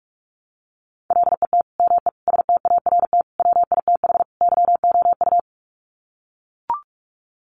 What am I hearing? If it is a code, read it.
Morse code, heard as a keyed tone, about 44 wpm, 700 Hz. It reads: LET ME START WITH YOU
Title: Sets of 5 Words Encore 45wpm